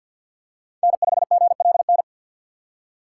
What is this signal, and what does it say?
Morse code, heard as a keyed tone, about 47 wpm, 690 Hz. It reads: N5GLN
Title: USA Call Signs - 5 Characters - 6x Character Spacing - 50wpm